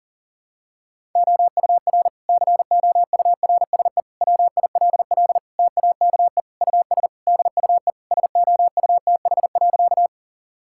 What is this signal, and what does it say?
Morse code, heard as a keyed tone, about 40 wpm, 700 Hz. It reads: OUR COURSE WILL TAKE US DUE SOUTH.
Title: Sentences from Top 1000 Words - Review Part 6 - 40wpm